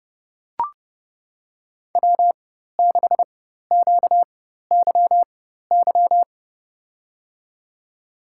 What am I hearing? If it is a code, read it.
Morse code, heard as a keyed tone, about 30 wpm, spaced as for 14 wpm, 700 Hz. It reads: W6QYY
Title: USA Call Signs - 4x Character Spacing - 30wpm